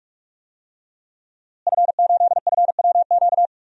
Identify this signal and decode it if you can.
Morse code, heard as a keyed tone, about 46 wpm, 700 Hz. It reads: F8FWQ
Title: DXCC-100 Call Signs - 5 Characters - Repeated 3x - 45wpm